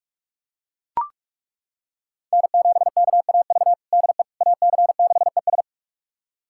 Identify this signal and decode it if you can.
Morse code, heard as a keyed tone, about 41 wpm, 700 Hz. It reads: N7KAV DE AC6ES
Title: USA Callsign DE Callsign - 5 Characters - 45wpm